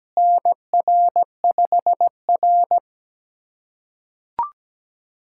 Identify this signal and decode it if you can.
Morse code, heard as a keyed tone, about 17 wpm, 700 Hz. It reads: NR5R
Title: USA Callsign DE Callsign - 4 Characters - 17wpm